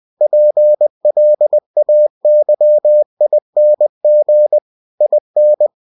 Morse code, heard as a keyed tone, about 20 wpm, 600 Hz. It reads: PLAYING IN